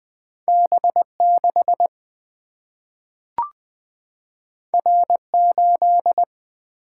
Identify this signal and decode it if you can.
Morse code, heard as a keyed tone, about 20 wpm, 700 Hz. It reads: B6 R8
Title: Single-Letter Single-Number 20wpm